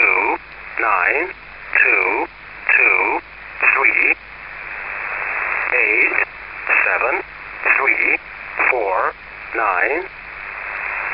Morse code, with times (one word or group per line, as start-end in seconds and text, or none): none